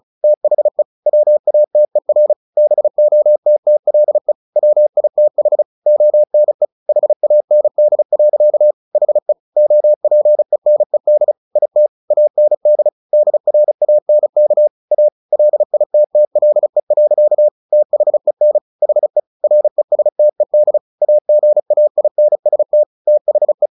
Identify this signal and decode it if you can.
Morse code, heard as a keyed tone, about 35 wpm, 600 Hz. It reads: THE WATER BOTTLE WITH ONE HAND. HE OPENED IT AND DRANK A LITTLE. THEN HE RESTED AGAINST THE
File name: the old man and the sea / the old man and the sea 1 35wpm